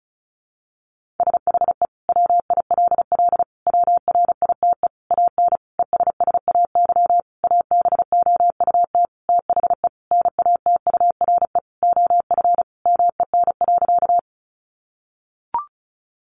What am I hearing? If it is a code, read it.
Morse code, heard as a keyed tone, about 34 wpm, 700 Hz. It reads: SHE WILL WRITE AN ESSAY ABOUT THE NATURE OF MEN.
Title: Sentences from Top 1500 Words 35wpm